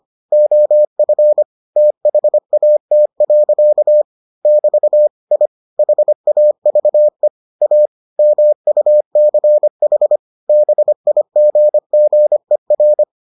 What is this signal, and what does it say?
Morse code, heard as a keyed tone, about 25 wpm, 600 Hz. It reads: OF THAT. <BT> I HAVE A MUCH BIGGER